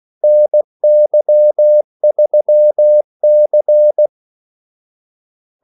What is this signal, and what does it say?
Morse code, heard as a keyed tone, about 16 wpm, 600 Hz. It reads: NY3C
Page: Kaldesignaler 16 wpm